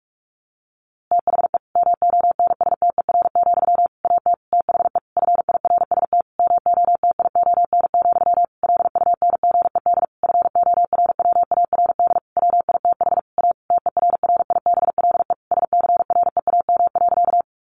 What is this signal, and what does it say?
Morse code, heard as a keyed tone, about 44 wpm, 700 Hz. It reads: THE MONSTER, AT THE FIRST MOTION, LUNGED FORWARD WITH A TERRIBLE SCREAM.
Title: A Sound Of Thunder 45wpm